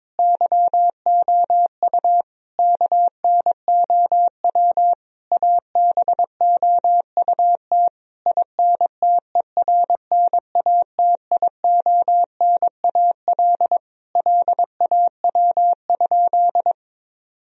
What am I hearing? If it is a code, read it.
Morse code, heard as a keyed tone, about 22 wpm, 700 Hz. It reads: YOU KNOW ABOUT INTERNATIONAL LAW?
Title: Sentences from Top 1000 Words - Review Part 4 - 22wpm